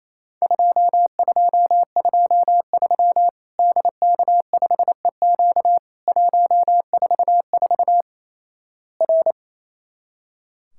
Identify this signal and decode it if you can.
Morse code, heard as a keyed tone, about 28 wpm, 700 Hz. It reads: 2223 DK5EQ 144 R
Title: Worked All Europe DX Contest - QTC Exchange - 28wpm